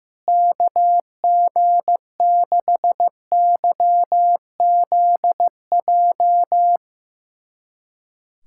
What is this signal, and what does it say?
Morse code, heard as a keyed tone, about 15 wpm, 700 Hz. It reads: KG6YZJ
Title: USA Callsign DE Callsign - 6 Characters - 15wpm